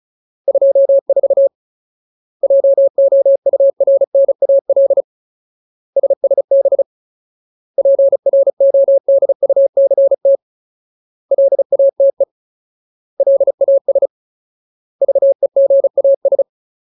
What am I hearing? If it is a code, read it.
Morse code, heard as a keyed tone, about 34 wpm, 550 Hz. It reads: 24 JOURNAL SSB PRODUCT LATE LAS VEGAS
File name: Ultimate Word List / 35WPM Ultimate